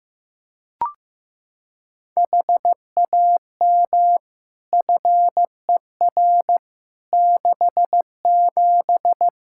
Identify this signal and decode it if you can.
Morse code, heard as a keyed tone, about 15 wpm, 700 Hz. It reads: HAM FER 67
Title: Practice QSO - Exchange 3 - More Info - 15wpm